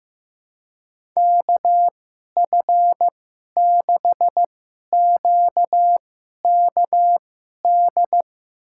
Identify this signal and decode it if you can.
Morse code, heard as a keyed tone, about 15 wpm, spaced as for 11 wpm, 700 Hz. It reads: KF6QKD
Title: USA Call Signs - 6 Characters - 2x Character Spacing - 15wpm